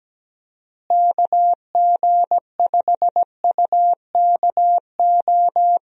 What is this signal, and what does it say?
Morse code, heard as a keyed tone, about 17 wpm, 700 Hz. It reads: KG5UKO
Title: USA Call Signs - 6 Characters - Repeated 3x - 17wpm